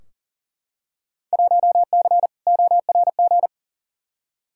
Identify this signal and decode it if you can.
Morse code, heard as a keyed tone, about 40 wpm, 700 Hz. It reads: 1C ORG
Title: Field Day Exchanges - 40wpm